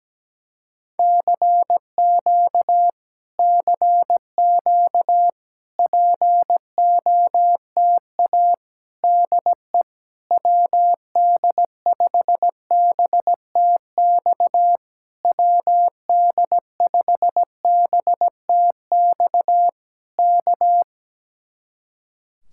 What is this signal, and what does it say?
Morse code, heard as a keyed tone, about 17 wpm, 700 Hz. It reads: CQ CQ POTA DE WD5BTX WD5BTX K
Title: Speed-Racing Common QSO Phrases - 17wpm